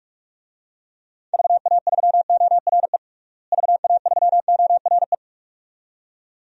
Morse code, heard as a keyed tone, about 47 wpm, 700 Hz. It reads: VA3ORE VA3ORE
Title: DXCC-100 Call Signs - 6 Characters - Repeated 2x - 45wpm